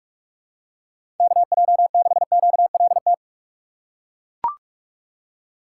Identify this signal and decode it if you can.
Morse code, heard as a keyed tone, about 42 wpm, 700 Hz. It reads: KJ6QLT
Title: USA Call Signs - 6 Characters - 3x Character Spacing - 45wpm